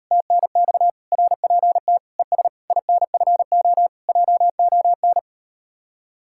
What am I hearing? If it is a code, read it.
Morse code, heard as a keyed tone, about 38 wpm, 700 Hz. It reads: TNX RPT ES INFO JON